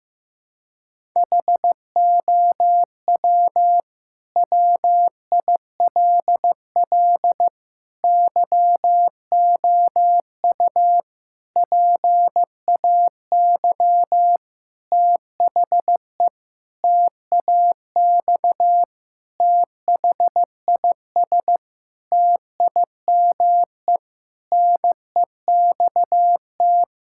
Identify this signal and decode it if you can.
Morse code, heard as a keyed tone, about 15 wpm, 700 Hz. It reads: HOW WILL YOU PAY THE TAX THIS TIME NEXT
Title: Sentences from Top 1000 Words - Review Part 5 - 15wpm